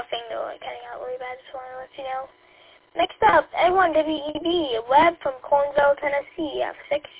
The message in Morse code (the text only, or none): none